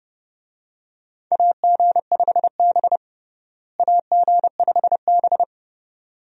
Morse code, heard as a keyed tone, about 32 wpm, 700 Hz. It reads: AG5B AG5B
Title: USA Call Signs - 4 Characters - Repeated 2x - 30wpm